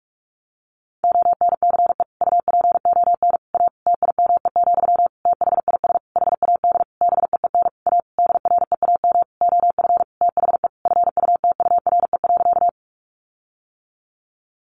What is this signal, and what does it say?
Morse code, heard as a keyed tone, about 44 wpm, 700 Hz. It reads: ONCE UPON A TIME, THIS HAD BEEN A DREAM OF THE FUTURE.
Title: Sentences from Top 1500 Words 45wpm